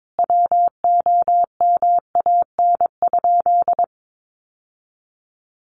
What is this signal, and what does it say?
Morse code, heard as a keyed tone, about 22 wpm, 700 Hz. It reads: WOMAN?
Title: Sentences from Top 1100 Words 22wpm